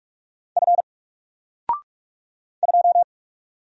Morse code, heard as a keyed tone, about 45 wpm, 700 Hz. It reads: F 2
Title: Single Letter-Number Alt 45wpm